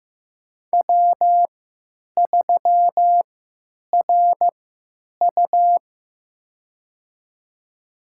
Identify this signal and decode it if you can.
Morse code, heard as a keyed tone, about 15 wpm, spaced as for 9 wpm, 700 Hz. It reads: W3RU
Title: USA Call Signs - 4 Characters - 3x Character Spacing - 15wpm